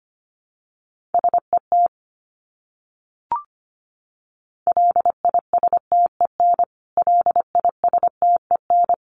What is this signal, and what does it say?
Morse code, heard as a keyed tone, about 25 wpm, 700 Hz. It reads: SET LISTEN LISTEN
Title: Top 500 Words Repeated 25wpm